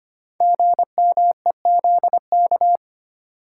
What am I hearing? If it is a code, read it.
Morse code, heard as a keyed tone, about 25 wpm, 700 Hz. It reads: GMEZK
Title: Code Groups - 5 Letter:Numbers - 25wpm